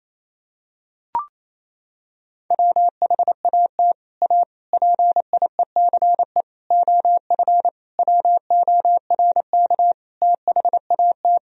Morse code, heard as a keyed tone, about 28 wpm, 700 Hz. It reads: WHAT A PIECE OF WORK THAT
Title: Sentences from Top 1000 Words - Review Part 2 - 28wpm